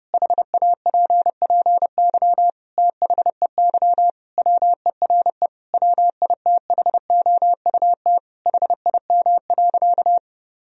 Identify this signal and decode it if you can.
Morse code, heard as a keyed tone, about 30 wpm, 700 Hz. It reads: HAPPY THEY WERE WITHOUT HIM.
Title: Sentences from Top 1000 Words - Review Part 3 - 30wpm